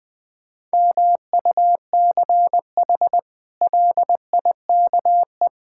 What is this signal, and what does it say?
Morse code, heard as a keyed tone, about 20 wpm, 700 Hz. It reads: MUCH LIKE
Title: Sentences with 8-Word Phrases - 20wpm